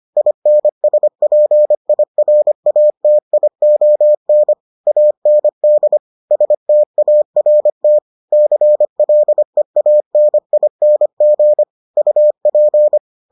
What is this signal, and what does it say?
Morse code, heard as a keyed tone, about 25 wpm, 600 Hz. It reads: INSPIRATION AND START CLEANING UP